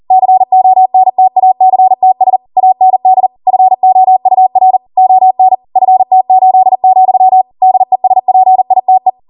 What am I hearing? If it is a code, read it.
Morse code, heard as a keyed tone, about 40 wpm, 750 Hz. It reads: CONTACTS AND FOUR ON FT8, DESPITE